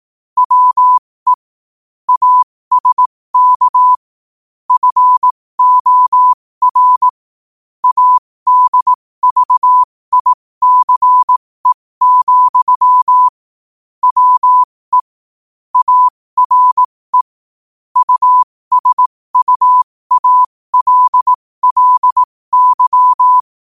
Morse code, heard as a keyed tone, about 18 wpm, spaced as for 15 wpm, 1000 Hz. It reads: WE ASK FOR ADVICE, WE ARE USUALLY
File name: qotdfiles / 15wpm